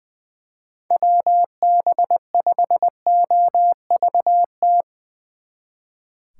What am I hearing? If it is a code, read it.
Morse code, heard as a keyed tone, about 22 wpm, 700 Hz. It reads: WB5OVT
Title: Speed Racing - USA Call Signs 20wpm